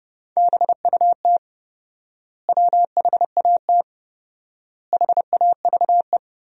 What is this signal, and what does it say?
Morse code, heard as a keyed tone, about 32 wpm, 700 Hz. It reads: BUT WHAT HAVE
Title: 4x Word-Spacing - Sentences from Top 100 Words - 30wpm